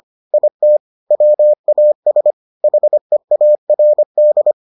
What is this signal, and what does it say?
Morse code, heard as a keyed tone, about 25 wpm, 600 Hz. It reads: IT WAS HEARD